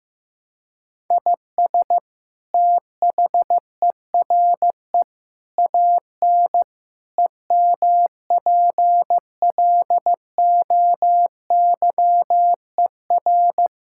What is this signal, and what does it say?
Morse code, heard as a keyed tone, about 15 wpm, 700 Hz. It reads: IS THERE AN EMPLOYER
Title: Sentences from Top 1600 Words 15wpm